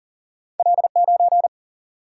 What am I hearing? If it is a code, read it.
Morse code, heard as a keyed tone, about 38 wpm, 700 Hz. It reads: L9
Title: Single-Letter Single-Number 40wpm